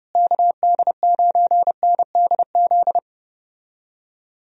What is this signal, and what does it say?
Morse code, heard as a keyed tone, about 31 wpm, 700 Hz. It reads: KD9NDZ
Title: USA Call Signs - 6 Characters - Repeated 3x - 30wpm